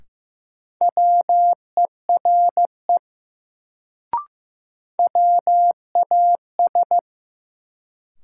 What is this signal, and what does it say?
Morse code, heard as a keyed tone, about 15 wpm, 700 Hz. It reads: WERE WAS
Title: Sentences from Top 100 Words - New Word Drill - 15wpm